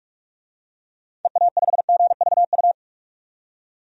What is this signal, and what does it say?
Morse code, heard as a keyed tone, about 46 wpm, 700 Hz. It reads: EA5GVU
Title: DXCC-100 Call Signs - 6 Characters - 45wpm